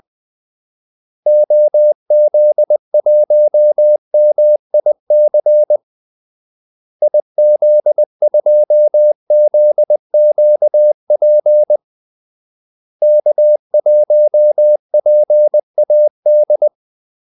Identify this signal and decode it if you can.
Morse code, heard as a keyed tone, about 20 wpm, 600 Hz. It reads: OZ1MIC IZ2ZQP K1PAD
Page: Kaldesignaler 20 wpm